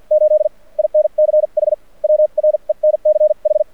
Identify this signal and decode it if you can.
Morse code, heard as a keyed tone, about 52 wpm, spaced as for 41 wpm, 610 Hz. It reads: 8 INCH WRENCH
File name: archived / CRSnet